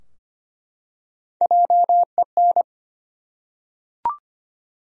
Morse code, heard as a keyed tone, about 25 wpm, 700 Hz. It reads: JEN
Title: USA Common Names - 25wpm